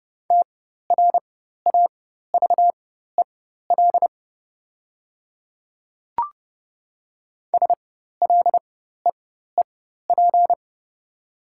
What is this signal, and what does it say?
Morse code, heard as a keyed tone, about 30 wpm, spaced as for 14 wpm, 700 Hz. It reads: TRAVEL SLEEP
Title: 4x Character Spacing - Top 500 Words - 30wpm